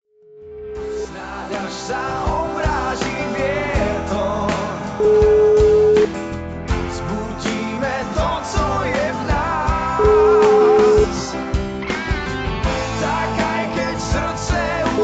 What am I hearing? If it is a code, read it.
Ringback tone (Europe, 425 Hz), with noise.